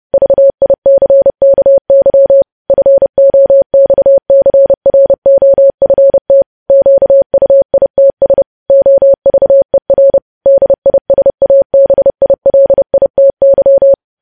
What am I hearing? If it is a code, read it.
Morse code, heard as a keyed tone, about 30 wpm, 550 Hz. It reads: VICKY FOXCROFT QUITS OVER DISABILITY
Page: news headlines in morse code at 30 wpm